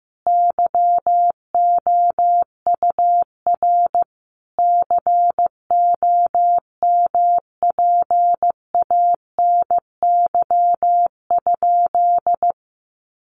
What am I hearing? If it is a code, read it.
Morse code, heard as a keyed tone, about 15 wpm, 700 Hz. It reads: YOUR COMPANY?
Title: Sentences from Top 600 Words 15wpm